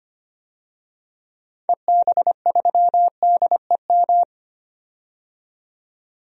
Morse code, heard as a keyed tone, about 25 wpm, 700 Hz. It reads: EB3DEM
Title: DXCC-100 Call Signs - 6 Characters - 25wpm